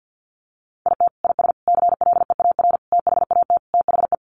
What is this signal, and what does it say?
Morse code, heard as a keyed tone, about 50 wpm, 700 Hz. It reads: IT IS CLEAR THAT THE
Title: Sets of 5 Words Encore 50wpm